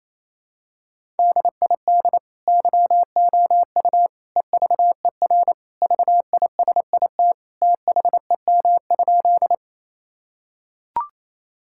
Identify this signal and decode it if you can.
Morse code, heard as a keyed tone, about 28 wpm, 700 Hz. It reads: DID YOU EVER VISIT THEM?
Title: Sentences from Top 1000 Words - Review Part 6 - 28wpm